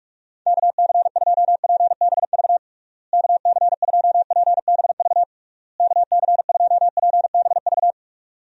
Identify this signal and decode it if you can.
Morse code, heard as a keyed tone, about 46 wpm, 700 Hz. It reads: KC2PBV KC2PBV KC2PBV
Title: USA Call Signs - 6 Characters - Repeated 3x - 45wpm